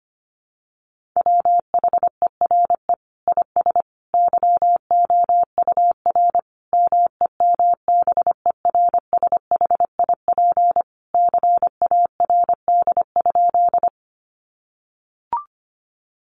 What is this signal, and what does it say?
Morse code, heard as a keyed tone, about 25 wpm, 700 Hz. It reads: WHERE IS YOUR MEMBERSHIP CARD?
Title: Sentences from Top 1400 Words 25wpm